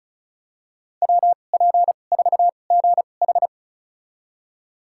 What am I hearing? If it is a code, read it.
Morse code, heard as a keyed tone, about 34 wpm, spaced as for 25 wpm, 700 Hz. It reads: WP4GH
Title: USA Call Signs - 5 Characters - 2x Character Spacing - 35wpm